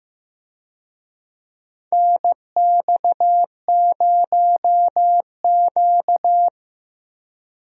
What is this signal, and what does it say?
Morse code, heard as a keyed tone, about 15 wpm, 700 Hz. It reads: NX0Q
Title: USA Call Signs - 4 Characters - Repeated 3x - 15wpm